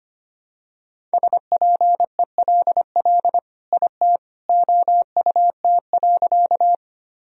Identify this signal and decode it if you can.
Morse code, heard as a keyed tone, about 25 wpm, 700 Hz. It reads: SPELL IT OUT.
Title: Sentences from Top 1000 Words - Review Part 1 - 25wpm